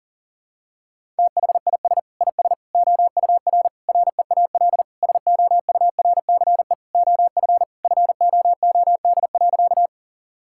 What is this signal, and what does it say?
Morse code, heard as a keyed tone, about 37 wpm, 700 Hz. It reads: THIS IS OUR REAL SOURCE OF FOOD.
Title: Sentences from Top 1000 Words - Review Part 5 - 40wpm